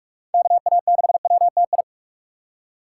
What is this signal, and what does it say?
Morse code, heard as a keyed tone, about 46 wpm, 700 Hz. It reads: KA6WTI
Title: USA Call Signs - 2x Character Spacing - 45wpm